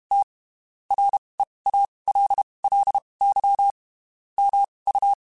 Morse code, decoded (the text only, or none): T REALLY MU